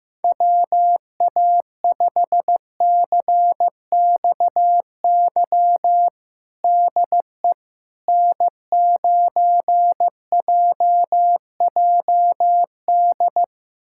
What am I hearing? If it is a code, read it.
Morse code, heard as a keyed tone, about 15 wpm, 700 Hz. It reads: WA5CXY DE N9JJD
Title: USA Callsign DE Callsign 15wpm